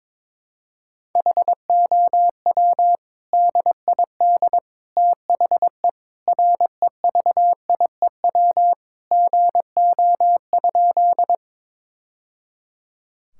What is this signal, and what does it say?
Morse code, heard as a keyed tone, about 22 wpm, 700 Hz. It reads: HOW DID THE REVIEW GO?
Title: Sentences from Top 1000 Words - Review Part 6 - 22wpm